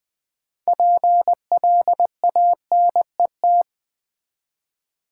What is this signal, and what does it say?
Morse code, heard as a keyed tone, about 20 wpm, 700 Hz. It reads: PLANET
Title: 6 Letter Words - First 3 Letters Given - 20wpm